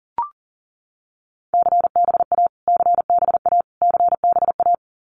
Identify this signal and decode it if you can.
Morse code, heard as a keyed tone, about 40 wpm, 700 Hz. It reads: CBA CBA CBA
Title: Abbreviations Repeated 3x - 40wpm